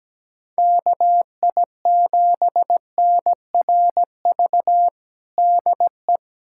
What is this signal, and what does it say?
Morse code, heard as a keyed tone, about 17 wpm, 700 Hz. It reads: KI7NRV DE
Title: USA Callsign DE Callsign 17wpm